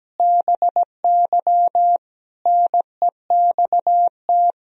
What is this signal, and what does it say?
Morse code, heard as a keyed tone, about 17 wpm, 700 Hz. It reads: BY NEXT
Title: Sentences from Top 1000 Words - Review Part 3 - 17wpm